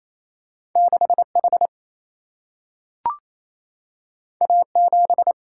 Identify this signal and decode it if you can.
Morse code, heard as a keyed tone, about 28 wpm, 700 Hz. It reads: 6H A7
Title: Single-Letter Single-Number 28wpm